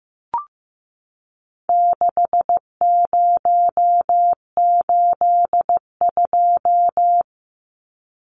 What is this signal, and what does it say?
Morse code, heard as a keyed tone, about 15 wpm, 700 Hz. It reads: 6082
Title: Numbers - 4-Digit 15wpm